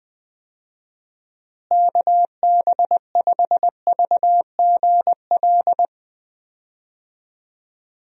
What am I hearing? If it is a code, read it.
Morse code, heard as a keyed tone, about 20 wpm, 700 Hz. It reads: KB5VGL
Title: USA Call Signs - 6 Characters - Repeated 3x - 20wpm